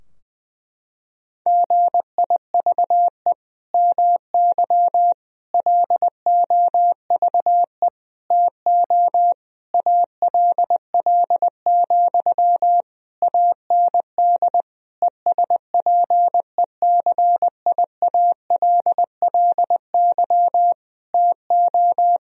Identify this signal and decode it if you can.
Morse code, heard as a keyed tone, about 20 wpm, 700 Hz. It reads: GIVE MY LOVE TO ALL, AND ESPECIALLY TO
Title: Sentences from Top 1000 Words - Review Part 4 - 20wpm